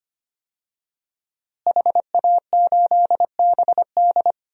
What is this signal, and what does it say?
Morse code, heard as a keyed tone, about 25 wpm, 700 Hz. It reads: HA8BD
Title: DXCC-100 Call Signs - 5 Characters - Repeated 2x - 25wpm